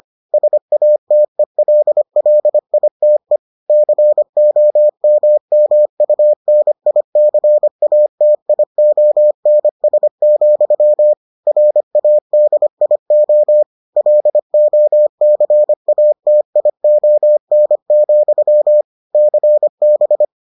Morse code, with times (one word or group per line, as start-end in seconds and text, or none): 0.3-3.4 SATELLITE
3.7-11.1 COMMUNICATIONS,
11.5-13.6 RADIO
14.0-18.8 LOCATION,
19.1-20.2 CB